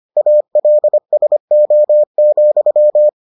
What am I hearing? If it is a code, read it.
Morse code, heard as a keyed tone, about 25 wpm, 600 Hz. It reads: ALSO,